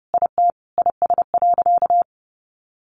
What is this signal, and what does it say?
Morse code, heard as a keyed tone, about 30 wpm, 700 Hz. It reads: IT IS.
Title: Sentences from Top 200 Words 30wpm